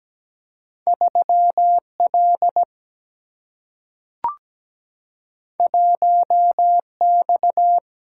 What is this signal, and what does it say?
Morse code, heard as a keyed tone, about 17 wpm, 700 Hz. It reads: 3L 1X
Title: Single-Letter Single-Number 17wpm